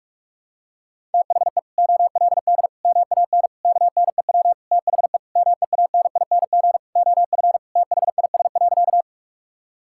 Morse code, heard as a keyed tone, about 41 wpm, 700 Hz. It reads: THE OLD MAN KNEW THE MEANING OF THIS.
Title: Sentences from Top 1000 Words - Review Part 6 - 45wpm